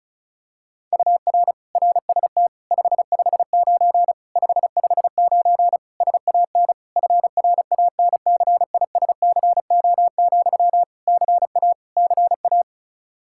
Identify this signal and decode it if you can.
Morse code, heard as a keyed tone, about 36 wpm, 700 Hz. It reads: UR RST 559 559 SAN FRANCISCO, CA CA
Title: Practice QSO - Exchange 1 - Beginning - 35wpm